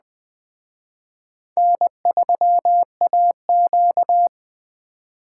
Morse code, heard as a keyed tone, about 20 wpm, 700 Hz. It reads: N3AQ
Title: Speed Racing - USA Call Signs 17wpm